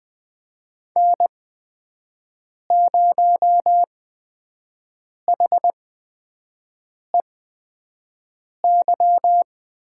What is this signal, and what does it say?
Morse code, heard as a keyed tone, about 20 wpm, spaced as for 5 wpm, 700 Hz. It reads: N0HEY
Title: USA Call Signs - 5 Characters - 8x Character Spacing - 20wpm